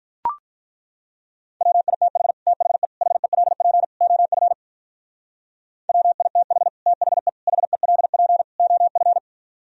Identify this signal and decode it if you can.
Morse code, heard as a keyed tone, about 43 wpm, 700 Hz. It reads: WITH THE HELP OF WITH THE HELP OF
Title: Speed-Racing - Sets of 4 Words 35wpm